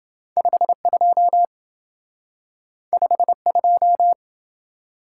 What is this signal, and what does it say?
Morse code, heard as a keyed tone, about 27 wpm, 700 Hz. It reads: 52 52